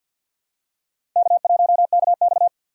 Morse code, heard as a keyed tone, about 53 wpm, 700 Hz. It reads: K1KX
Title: USA Call Signs - 4 Characters - 2x Character Spacing - 50wpm